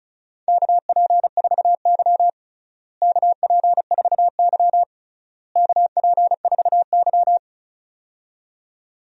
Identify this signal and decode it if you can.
Morse code, heard as a keyed tone, about 34 wpm, 700 Hz. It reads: KP4Y KP4Y KP4Y
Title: DXCC-100 Call Signs - 4 Characters - Repeated 3x - 35wpm